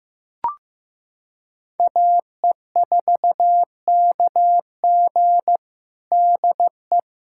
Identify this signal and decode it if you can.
Morse code, heard as a keyed tone, about 15 wpm, 700 Hz. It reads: AE4KG DE
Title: USA Callsign DE Callsign - 5 Characters - 15wpm